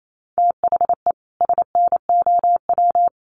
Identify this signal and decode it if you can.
Morse code, heard as a keyed tone, about 28 wpm, 700 Hz. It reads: THE SNOW
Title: Sentences from Top 500 Words 28wpm